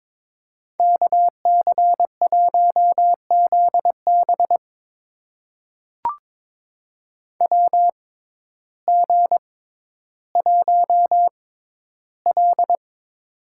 Morse code, heard as a keyed tone, about 22 wpm, spaced as for 12 wpm, 700 Hz. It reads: KC1ZB WG1L
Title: USA Call Signs - 6x Character Spacing - 22wpm